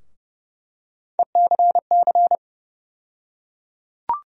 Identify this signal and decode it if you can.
Morse code, heard as a keyed tone, about 30 wpm, 700 Hz. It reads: ECC
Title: USA Call Sign Postfixes - 3 Letters 30wpm